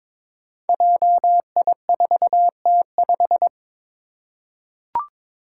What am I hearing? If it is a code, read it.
Morse code, heard as a keyed tone, about 22 wpm, 700 Hz. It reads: JI4T5
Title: Five Challenging Letter-Number Combinations 22wpm